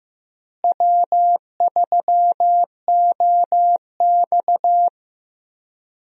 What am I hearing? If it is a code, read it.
Morse code, heard as a keyed tone, about 15 wpm, 700 Hz. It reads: W3OX
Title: USA Call Signs - 4 Characters - Repeated 2x - 15wpm